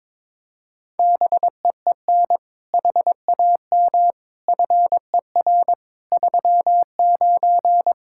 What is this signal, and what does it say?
Morse code, heard as a keyed tone, about 22 wpm, 700 Hz. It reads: BEEN HAM FER 39
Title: Practice QSO - Exchange 3 - More Info - 22wpm